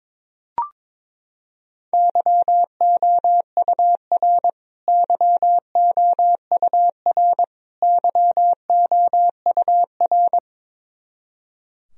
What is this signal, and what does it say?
Morse code, heard as a keyed tone, about 22 wpm, 700 Hz. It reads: YOUR YOUR YOUR
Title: Top 100 Words Repeated 22wpm